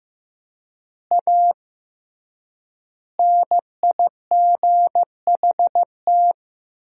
Morse code, heard as a keyed tone, about 16 wpm, 700 Hz. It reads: A NIGHT
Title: 3x Word-Spacing - Sentences from Top 300 Words - 15wpm